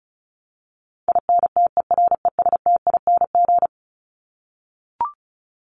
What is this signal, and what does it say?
Morse code, heard as a keyed tone, about 34 wpm, 700 Hz. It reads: INTERESTING
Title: ING Suffix - Top 100 Words with Base Words 35wpm